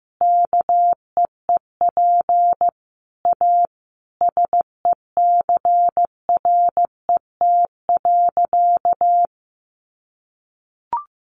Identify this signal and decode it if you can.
Morse code, heard as a keyed tone, about 14 wpm, 700 Hz. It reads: KEEP A SECRET.
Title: Sentences from Top 1300 Words 15wpm